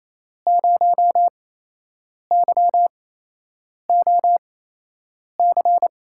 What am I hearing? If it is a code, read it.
Morse code, heard as a keyed tone, about 28 wpm, spaced as for 8 wpm, 700 Hz. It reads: 0YOC
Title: USA Call Signs - 5 Characters - 8x Character Spacing - 28wpm